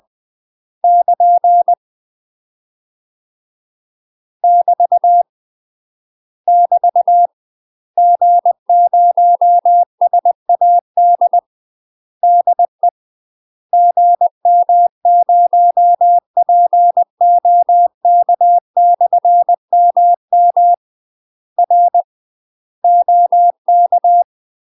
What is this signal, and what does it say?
Morse code, heard as a keyed tone, about 20 wpm, 700 Hz. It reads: <KN> <BT> <BT> G0SAD DE GM0POK/MM R OK
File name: UK MASTER QSOs TRAINING / 20WPM / UK MASTER QSOs 0013